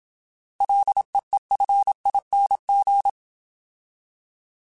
Morse code, heard as a keyed tone, about 27 wpm, 800 Hz. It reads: LEEFING